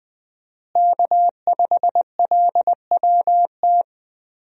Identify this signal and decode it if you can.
Morse code, heard as a keyed tone, about 20 wpm, 700 Hz. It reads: K5LWT
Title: USA Call Signs - 5 Characters - Repeated 3x - 20wpm